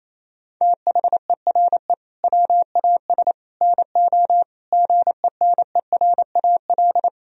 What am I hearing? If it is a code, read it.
Morse code, heard as a keyed tone, about 28 wpm, 700 Hz. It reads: THERE WAS NO GENERAL
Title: Sentences from Top 1000 Words - Review Part 4 - 28wpm